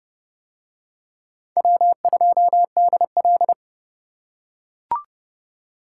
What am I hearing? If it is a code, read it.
Morse code, heard as a keyed tone, about 27 wpm, 700 Hz. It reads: W2DL